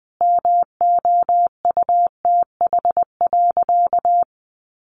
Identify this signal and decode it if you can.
Morse code, heard as a keyed tone, about 20 wpm, 700 Hz. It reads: MOUTH.
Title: Sentences from Top 1500 Words 20wpm